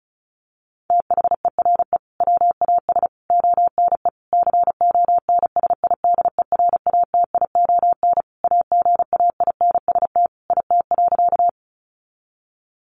Morse code, heard as a keyed tone, about 34 wpm, 700 Hz. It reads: THERE WAS ONE CONSIDERATION AGAINST IT.
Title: Sentences from Top 1500 Words 35wpm